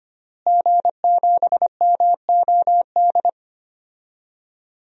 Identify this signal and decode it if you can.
Morse code, heard as a keyed tone, about 25 wpm, 700 Hz. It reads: G7MOD
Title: DXCC-100 Call Signs - 5 Characters - Repeated 3x - 25wpm